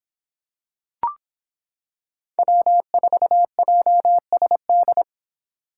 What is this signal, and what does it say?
Morse code, heard as a keyed tone, about 26 wpm, 700 Hz. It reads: W4JSD